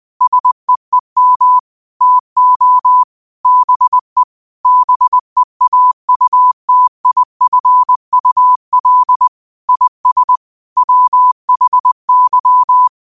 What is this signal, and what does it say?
Morse code, heard as a keyed tone, about 20 wpm, 1000 Hz. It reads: SEEM TO BE BEAUTIFUL IS WHY
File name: qotdfiles / 20wpm